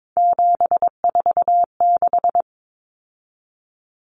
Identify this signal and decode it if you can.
Morse code, heard as a keyed tone, about 22 wpm, 700 Hz. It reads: 746
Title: Common Cut Numbers - 3-Digit 22wpm